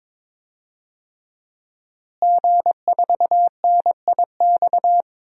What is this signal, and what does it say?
Morse code, heard as a keyed tone, about 22 wpm, 700 Hz. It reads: G4NIX
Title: DXCC-100 Call Signs - 5 Characters - Repeated 3x - 22wpm